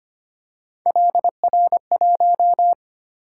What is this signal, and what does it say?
Morse code, heard as a keyed tone, about 25 wpm, 700 Hz. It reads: LR1
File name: Three Letter-Number Combinations - 25wpm